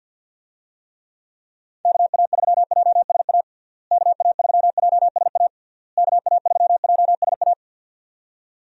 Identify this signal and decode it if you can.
Morse code, heard as a keyed tone, about 51 wpm, 700 Hz. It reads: KA3JSA KA3JSA KA3JSA
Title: USA Call Signs - 6 Characters - Repeated 3x - 50wpm